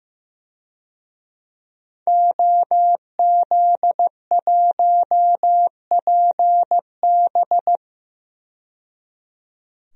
Morse code, heard as a keyed tone, about 15 wpm, 700 Hz. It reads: OZ1PB
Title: DXCC-100 Call Signs - 5 Characters - 15wpm